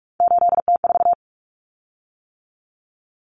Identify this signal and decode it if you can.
Morse code, heard as a keyed tone, about 44 wpm, 700 Hz. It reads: 8T4
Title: Common Cut Numbers - 3-Digit 45wpm